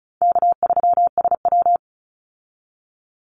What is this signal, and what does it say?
Morse code, heard as a keyed tone, about 34 wpm, 700 Hz. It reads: K3SW
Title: Easy Call Signs Repeated 3x - 35wpm